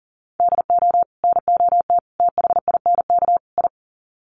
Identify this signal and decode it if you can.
Morse code, heard as a keyed tone, about 40 wpm, 700 Hz. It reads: DO NOT THINK I
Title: Sets of 4 Words 40wpm